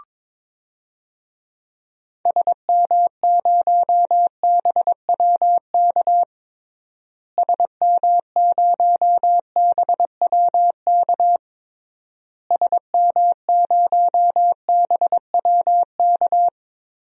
Morse code, heard as a keyed tone, about 22 wpm, 700 Hz. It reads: SM0BWK SM0BWK SM0BWK
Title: DXCC-100 Call Signs - 6 Characters - Repeated 3x - 22wpm